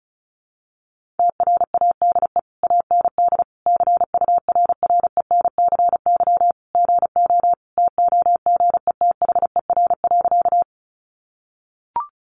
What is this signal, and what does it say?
Morse code, heard as a keyed tone, about 34 wpm, 700 Hz. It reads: TRADE AND CURRENCY GO TOGETHER.
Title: Sentences from Top 1500 Words 35wpm